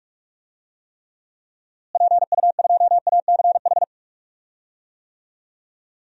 Morse code, heard as a keyed tone, about 46 wpm, 700 Hz. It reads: PU2ACH
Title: DXCC-100 Call Signs - 6 Characters - Repeated 3x - 45wpm